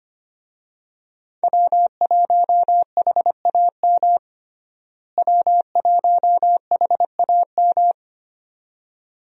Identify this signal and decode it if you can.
Morse code, heard as a keyed tone, about 25 wpm, 700 Hz. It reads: W1HAM W1HAM
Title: USA Call Signs - 5 Characters - Repeated 2x - 25wpm